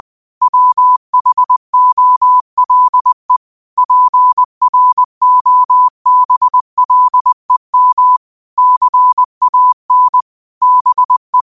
Morse code, heard as a keyed tone, about 20 wpm, 1000 Hz. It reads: WHOLE PROBLEM CAN BE
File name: qotdfiles / 20wpm